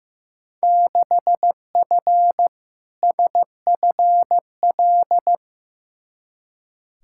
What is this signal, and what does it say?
Morse code, heard as a keyed tone, about 15 wpm, 700 Hz. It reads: 6F SFL
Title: Field Day Exchanges Encore - 15wpm